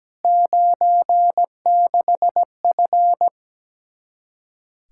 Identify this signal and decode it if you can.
Morse code, heard as a keyed tone, about 17 wpm, 700 Hz. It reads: 96F
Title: Practice QSO - Exchange 2 - Info - 17wpm